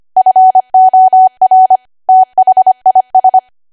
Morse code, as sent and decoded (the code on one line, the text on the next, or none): ..-. --- .-. / - .... .. ...
FOR THIS